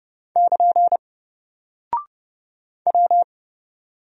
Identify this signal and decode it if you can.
Morse code, heard as a keyed tone, about 30 wpm, 700 Hz. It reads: <KN> W
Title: Single Character (Letter:Number:Punctuation:Prosign) 30wpm